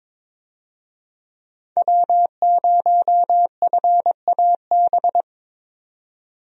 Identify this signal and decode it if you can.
Morse code, heard as a keyed tone, about 22 wpm, 700 Hz. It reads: W0FAB
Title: USA Call Signs - 5 Characters - 22wpm